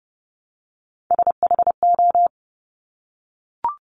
Morse code, heard as a keyed tone, about 30 wpm, 700 Hz. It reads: SHO
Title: Top Three Letter Combinations in Words 30wpm